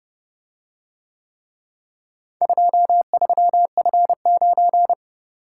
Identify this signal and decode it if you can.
Morse code, heard as a keyed tone, about 29 wpm, 700 Hz. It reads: 23F9